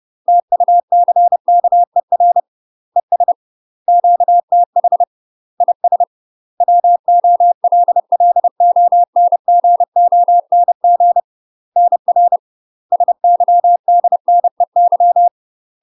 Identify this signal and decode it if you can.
Morse code, heard as a keyed tone, about 30 wpm, 700 Hz. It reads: TUCKER ES QTH IS WOLLONGONG NR SYDNEY
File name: UK MASTER QSOs TRAINING / 30WPM / UK MASTER QSOs 0015